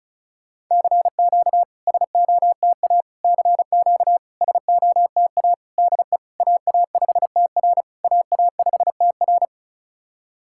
Morse code, heard as a keyed tone, about 36 wpm, 690 Hz. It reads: CQ SOTA CQ SOTA DE AA5TR AA5TR
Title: SOTA QSO Practice 35wpm